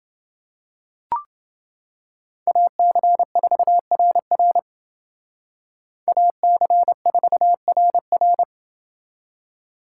Morse code, heard as a keyed tone, about 29 wpm, 700 Hz. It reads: AC4RR AC4RR